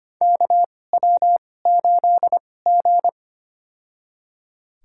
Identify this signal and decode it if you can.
Morse code, heard as a keyed tone, about 25 wpm, spaced as for 18 wpm, 700 Hz. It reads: KW8G
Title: USA Call Signs - 4 Characters - 2x Character Spacing - 25wpm